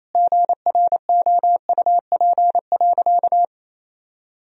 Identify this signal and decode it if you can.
Morse code, heard as a keyed tone, about 28 wpm, 700 Hz. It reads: GROUP.
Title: Sentences from Top 1000 Words - Review Part 1 - 28wpm